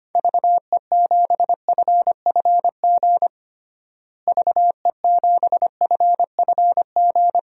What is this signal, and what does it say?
Morse code, heard as a keyed tone, about 25 wpm, 700 Hz. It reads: VE7FFG VE7FFG
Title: DXCC-100 Call Signs - 6 Characters - Repeated 2x - 25wpm